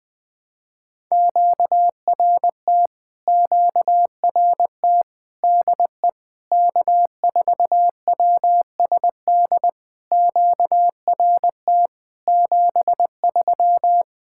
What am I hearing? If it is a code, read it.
Morse code, heard as a keyed tone, about 20 wpm, 690 Hz. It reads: QRT QRT DE K4WSD QRT 73
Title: POTA QSO Practice - 20wpm